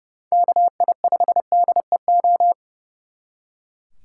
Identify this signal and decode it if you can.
Morse code, heard as a keyed tone, about 37 wpm, 700 Hz. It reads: KI5DEO
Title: Speed Racing - USA 2x3 Call Signs 30wpm